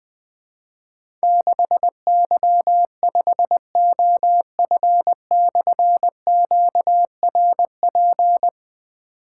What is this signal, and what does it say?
Morse code, heard as a keyed tone, about 20 wpm, 700 Hz. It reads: 6Y5OF/QRP
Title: Difficult Call Signs 20wpm